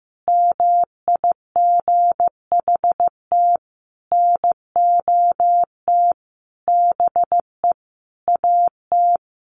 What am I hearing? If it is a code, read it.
Morse code, heard as a keyed tone, about 15 wpm, 700 Hz. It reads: MIGHT NOT BE AT
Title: Sentences from Top 1500 Words 15wpm